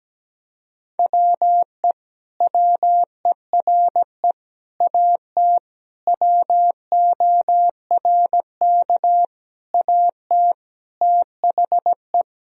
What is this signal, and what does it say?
Morse code, heard as a keyed tone, about 17 wpm, 700 Hz. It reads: WE WERE AT WORK AT THE
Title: Sentences from Top 1000 Words - Review Part 1 - 17wpm